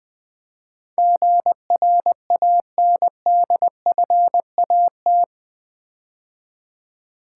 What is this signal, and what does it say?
Morse code, heard as a keyed tone, about 20 wpm, 700 Hz. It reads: GRANDFAT
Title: Build A Word - 4 Letters At A Time - 11-Letter Words - 20wpm